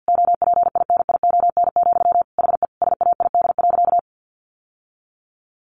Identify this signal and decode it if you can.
Morse code, heard as a keyed tone, about 48 wpm, 700 Hz. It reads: OPINION, HE SAID.
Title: Sentences from Top 1200 Words 50wpm